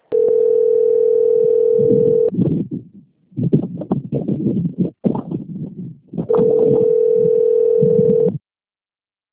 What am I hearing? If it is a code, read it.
Ringback tone (North America), 440 + 480 Hz.